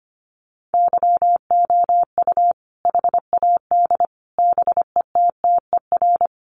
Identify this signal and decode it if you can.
Morse code, heard as a keyed tone, about 25 wpm, 700 Hz. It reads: YOU HAD BETTER
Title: Sentences from Top 700 Words 25wpm